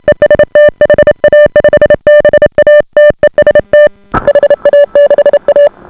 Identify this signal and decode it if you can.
Morse code, heard as a keyed tone, about 28 wpm, 590 Hz. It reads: EST HA5BA TEST HA6A